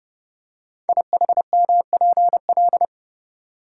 Morse code, heard as a keyed tone, about 30 wpm, 700 Hz. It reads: IHMPL
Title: Code Groups - 5 Letters - 30wpm